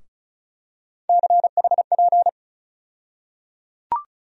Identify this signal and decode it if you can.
Morse code, heard as a keyed tone, about 36 wpm, 700 Hz. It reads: CHP
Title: USA Call Sign Postfixes - 3 Letters 35wpm